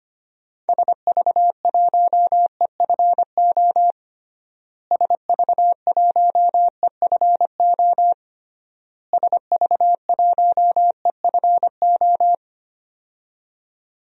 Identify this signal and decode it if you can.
Morse code, heard as a keyed tone, about 25 wpm, 700 Hz. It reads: SV1EFO SV1EFO SV1EFO
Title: DXCC-100 Call Signs - 6 Characters - Repeated 3x - 25wpm